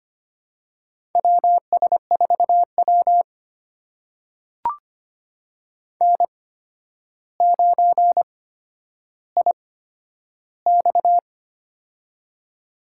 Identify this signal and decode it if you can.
Morse code, heard as a keyed tone, about 25 wpm, spaced as for 11 wpm, 700 Hz. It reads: WS4W N9IX
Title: USA Call Signs - 4 Characters - 8x Character Spacing - 25wpm